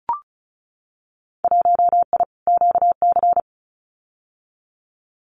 Morse code, heard as a keyed tone, about 34 wpm, 700 Hz. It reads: 1I QC
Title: Winter Field Day Exchanges 35wpm